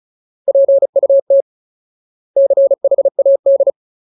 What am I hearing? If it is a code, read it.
Morse code, heard as a keyed tone, about 34 wpm, 550 Hz. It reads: PUT CHAD
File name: Ultimate Word List / 35WPM Ultimate